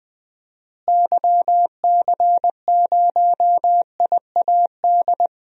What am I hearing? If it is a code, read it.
Morse code, heard as a keyed tone, about 19 wpm, 700 Hz. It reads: YC0IAD
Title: DXCC-100 Call Signs - 6 Characters - Repeated 2x - 20wpm